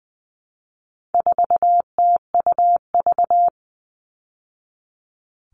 Morse code, heard as a keyed tone, about 20 wpm, 700 Hz. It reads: 4TUV
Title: All Cut Numbers - 4-Digit 20wpm